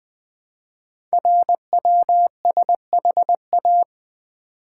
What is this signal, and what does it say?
Morse code, heard as a keyed tone, about 20 wpm, 700 Hz. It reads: RWSHA
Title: Code Groups - 5 Letters - 20wpm